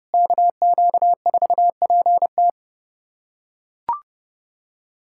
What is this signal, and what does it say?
Morse code, heard as a keyed tone, about 32 wpm, 700 Hz. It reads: KQ4PT
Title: USA Callsign DE Callsign - 5 Characters - 30wpm